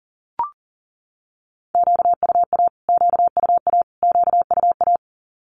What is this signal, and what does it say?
Morse code, heard as a keyed tone, about 40 wpm, 700 Hz. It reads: QUA QUA QUA
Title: Q-Codes Repeated 3x 40wpm